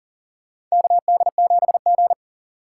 Morse code, heard as a keyed tone, about 40 wpm, 700 Hz. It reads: KD7G
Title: USA Call Signs - 4 Characters - 6x Character Spacing - 40wpm